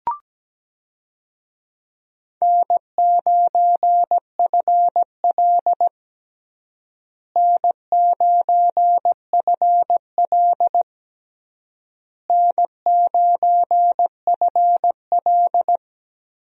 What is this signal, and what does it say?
Morse code, heard as a keyed tone, about 17 wpm, 700 Hz. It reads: N9FL N9FL N9FL
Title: USA Call Signs - 4 Characters - Repeated 3x - 17wpm